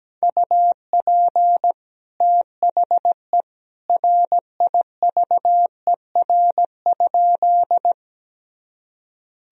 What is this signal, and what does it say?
Morse code, heard as a keyed tone, about 17 wpm, 700 Hz. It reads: UP THE RIVER?
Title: Sentences from Top 1000 Words - Review Part 1 - 17wpm